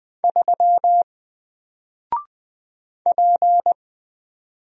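Morse code, heard as a keyed tone, about 20 wpm, 700 Hz. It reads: 3 P
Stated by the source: Single letter and number Morse drill at 20 wpm, from N4LCD.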